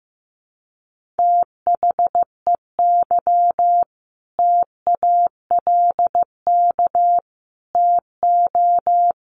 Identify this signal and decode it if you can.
Morse code, heard as a keyed tone, about 15 wpm, 700 Hz. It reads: THEY TALK TO
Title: Sentences from Top 1400 Words 15wpm